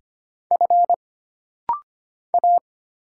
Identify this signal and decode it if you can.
Morse code, heard as a keyed tone, about 25 wpm, 700 Hz. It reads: F A
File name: Single Character (Letter:Number:Punctuation:Prosign) - Rapid-Fire - 25wpm